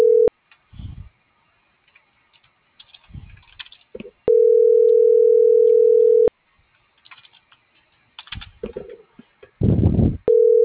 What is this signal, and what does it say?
Ringback tone (North America), 440 + 480 Hz.